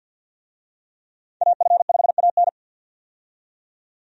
Morse code, heard as a keyed tone, about 50 wpm, 700 Hz. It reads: AF5AN
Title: USA Call Signs - 5 Characters - 50wpm